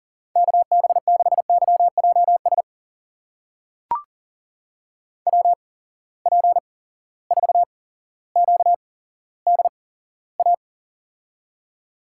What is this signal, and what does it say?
Morse code, heard as a keyed tone, about 40 wpm, spaced as for 17 wpm, 700 Hz. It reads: KB6YJS WP4QDA
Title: USA Call Signs - 6 Characters - 8x Character Spacing - 40wpm